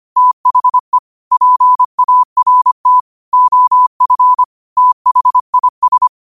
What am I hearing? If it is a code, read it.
Morse code, heard as a keyed tone, about 25 wpm, 1000 Hz. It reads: THE PART OF THIS
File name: qotdfiles / 25wpm